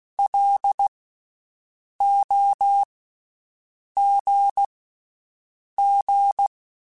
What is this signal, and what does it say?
Morse code, heard as a keyed tone, about 16 wpm, spaced as for 6 wpm, 800 Hz. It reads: LOGG
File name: Special INGs / Special INGs-mp3s / VerySlowToFaster Ing @15